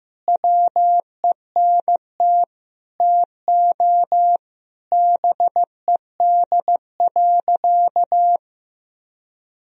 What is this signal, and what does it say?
Morse code, heard as a keyed tone, about 15 wpm, 700 Hz. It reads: WENT TO BED.
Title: Sentences from Top 1000 Words - Review Part 3 - 15wpm